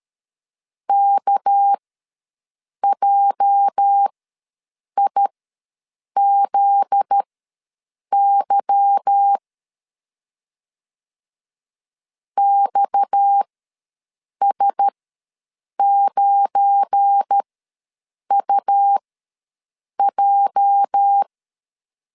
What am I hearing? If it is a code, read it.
Morse code, heard as a keyed tone, about 13 wpm, spaced as for 7 wpm, 800 Hz. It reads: KJIZY XS9UJ
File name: LEZIONI / 8 YJ45 / 8 YJ ws 12wpm 3342